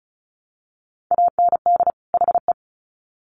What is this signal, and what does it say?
Morse code, heard as a keyed tone, about 34 wpm, 700 Hz. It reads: AND HE
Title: Sets of 2 Words 35wpm